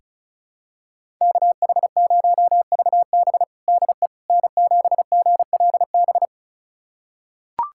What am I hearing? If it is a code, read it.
Morse code, heard as a keyed tone, about 36 wpm, 700 Hz. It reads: KH0VB DE N7GLB
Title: USA Callsign DE Callsign - 5 Characters - 35wpm